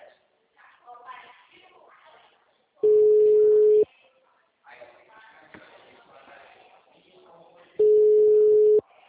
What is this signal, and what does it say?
Ringback tone (Europe, 425 Hz), with speech.